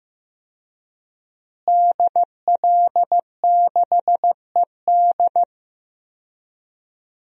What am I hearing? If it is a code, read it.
Morse code, heard as a keyed tone, about 15 wpm, 700 Hz. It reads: DL6ED
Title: DXCC-100 Call Signs - 5 Characters - Repeated 2x - 15wpm